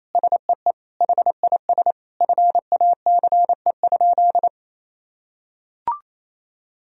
Morse code, heard as a keyed tone, about 28 wpm, 700 Hz. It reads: SEE HIS FACE?
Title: Sentences from Top 1000 Words - Review Part 2 - 28wpm